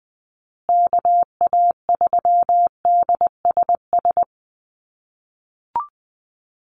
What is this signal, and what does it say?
Morse code, heard as a keyed tone, about 20 wpm, 700 Hz. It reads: KA3DSS
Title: USA Call Signs Repeated 3x 20wpm